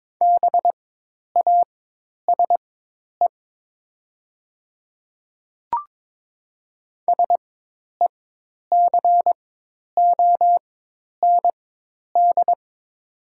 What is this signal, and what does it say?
Morse code, heard as a keyed tone, about 22 wpm, spaced as for 10 wpm, 690 Hz. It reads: BASE SECOND
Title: 4x Character Spacing - Top 300 Words - 22wpm